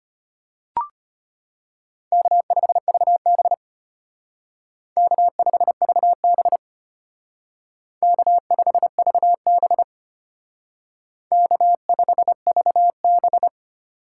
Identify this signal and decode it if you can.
Morse code, heard as a keyed tone, about 32 wpm, 700 Hz. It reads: K5VB K5VB K5VB K5VB